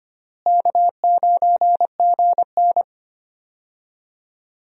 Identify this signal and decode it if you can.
Morse code, heard as a keyed tone, about 25 wpm, 700 Hz. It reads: K9GN
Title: USA Call Signs - 4 Characters - Repeated 3x - 25wpm